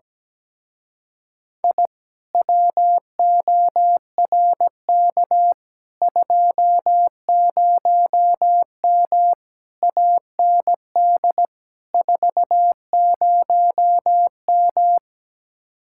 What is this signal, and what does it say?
Morse code, heard as a keyed tone, about 17 wpm, 700 Hz. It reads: I WORK 20M AND 40M
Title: Speed-Racing Common QSO Phrases Encore - 17wpm